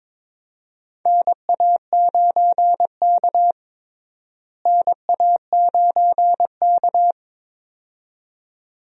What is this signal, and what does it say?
Morse code, heard as a keyed tone, about 22 wpm, 700 Hz. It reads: NA9K NA9K
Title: USA Call Signs - 4 Characters - Repeated 2x - 22wpm